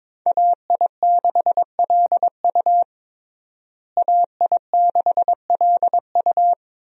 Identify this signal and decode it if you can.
Morse code, heard as a keyed tone, about 22 wpm, 700 Hz. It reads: AI6LU AI6LU
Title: USA Call Signs - 5 Characters - Repeated 3x - 22wpm